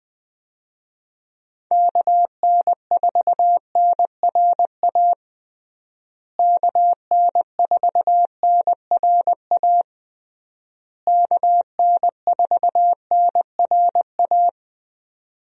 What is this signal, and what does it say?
Morse code, heard as a keyed tone, about 21 wpm, 700 Hz. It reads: KN4NRA KN4NRA KN4NRA
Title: USA Call Signs - 6 Characters - Repeated 3x - 20wpm